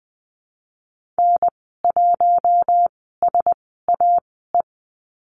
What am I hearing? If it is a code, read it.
Morse code, heard as a keyed tone, about 20 wpm, spaced as for 15 wpm, 700 Hz. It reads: N1SAE
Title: 055 Lesson 10g - Letter r - Call Signs 20wpm